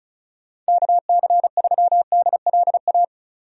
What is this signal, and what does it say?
Morse code, heard as a keyed tone, about 34 wpm, 700 Hz. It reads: KC3DLA
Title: USA Call Signs - 6 Characters - 35wpm